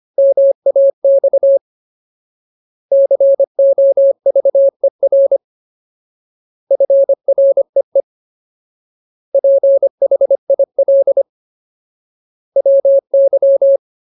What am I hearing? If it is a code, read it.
Morse code, heard as a keyed tone, about 25 wpm, 550 Hz. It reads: MAX COVER FREE PHIL WY
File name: Ultimate Word List / 25WPM Ultimate